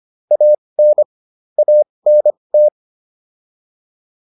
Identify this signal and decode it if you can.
Morse code, heard as a keyed tone, about 25 wpm, spaced as for 20 wpm, 600 Hz. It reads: AN ANT